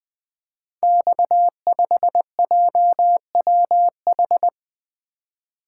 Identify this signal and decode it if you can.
Morse code, heard as a keyed tone, about 20 wpm, 700 Hz. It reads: X5JWH
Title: Code Groups - 5 Letter:Numbers - 20wpm